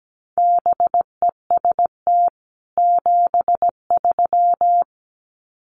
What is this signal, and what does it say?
Morse code, heard as a keyed tone, about 17 wpm, 700 Hz. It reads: BEST 73
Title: Common QSO Phrases 17wpm